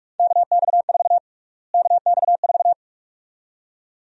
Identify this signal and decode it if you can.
Morse code, heard as a keyed tone, about 46 wpm, 700 Hz. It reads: KX4 KX4
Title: USA Call Sign Prefixes - Repeated 2x - 45wpm